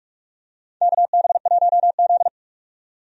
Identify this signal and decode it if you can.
Morse code, heard as a keyed tone, about 43 wpm, 700 Hz. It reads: KB1Z
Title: USA Call Signs - 4 Characters - 45wpm